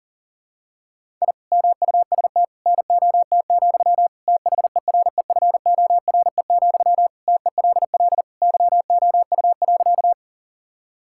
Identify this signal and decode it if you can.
Morse code, heard as a keyed tone, about 40 wpm, 700 Hz. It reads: I MUST NOT, THEREFORE, TELL YOU.
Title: Sentences from Top 1000 Words - Review Part 5 - 40wpm